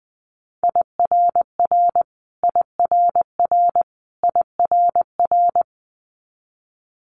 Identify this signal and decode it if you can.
Morse code, heard as a keyed tone, about 20 wpm, 700 Hz. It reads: IRR IRR IRR
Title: Multi-letter Phonemes Repeated 3x 20wpm